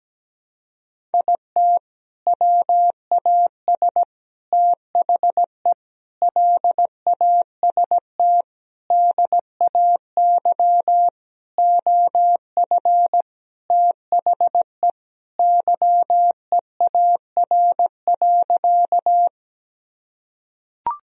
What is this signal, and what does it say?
Morse code, heard as a keyed tone, about 17 wpm, 700 Hz. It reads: IT WAS THE LAST DAY OF THE YEAR.
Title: Sentences from Top 1000 Words - Review Part 1 - 17wpm